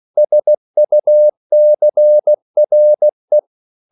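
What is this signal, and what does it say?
Morse code, heard as a keyed tone, about 16 wpm, 600 Hz. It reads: SUCRE
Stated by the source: CW practice of national capitals at 16 words per minute, from ON6ZQ.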